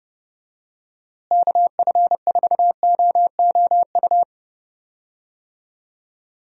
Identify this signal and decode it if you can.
Morse code, heard as a keyed tone, about 30 wpm, 700 Hz. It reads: KF4OOU
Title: USA Call Signs - 6 Characters - Repeated 2x - 30wpm